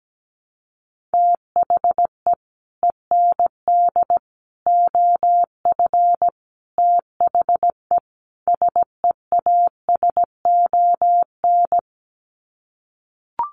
Morse code, heard as a keyed tone, about 17 wpm, 700 Hz. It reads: THE END OF THE SEASON
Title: Sets of 5 Words Encore 17wpm